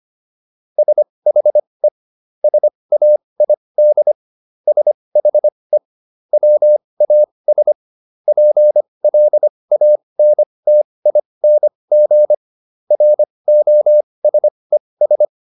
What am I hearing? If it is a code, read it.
Morse code, heard as a keyed tone, about 25 wpm, spaced as for 20 wpm, 600 Hz. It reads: SHE SAID SHE WAS PLANTING ROSES